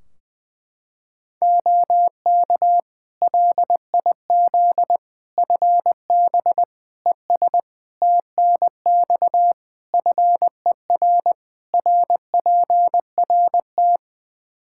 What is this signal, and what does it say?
Morse code, heard as a keyed tone, about 20 wpm, 700 Hz. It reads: OK LIZ FB ES TNX FER RPRT
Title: K1USN SST QSO Practice 20wpm